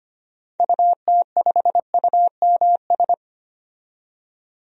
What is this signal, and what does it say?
Morse code, heard as a keyed tone, about 25 wpm, 700 Hz. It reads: UT5UMS
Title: DXCC-100 Call Signs - 6 Characters - 25wpm